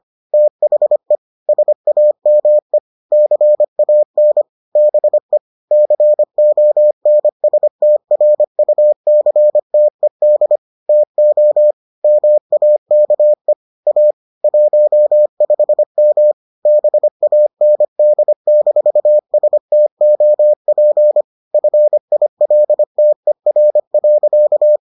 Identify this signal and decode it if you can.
Morse code, heard as a keyed tone, about 25 wpm, 600 Hz. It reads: THE SAME CAN BE CONSTRUCTED TO MAKE A 15M BAND-STOP FILTER.